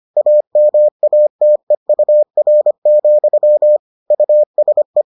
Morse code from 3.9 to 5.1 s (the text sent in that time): USE